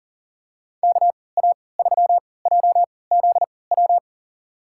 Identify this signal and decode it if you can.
Morse code, heard as a keyed tone, about 37 wpm, spaced as for 22 wpm, 700 Hz. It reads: KA3JZW
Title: USA Call Signs - 6 Characters - 3x Character Spacing - 40wpm